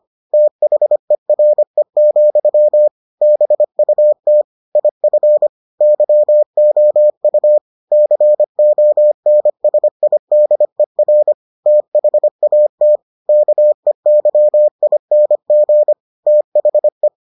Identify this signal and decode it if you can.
Morse code, heard as a keyed tone, about 25 wpm, 600 Hz. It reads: THERE, BUT IF YOU CONSIDER THAT KEYING THE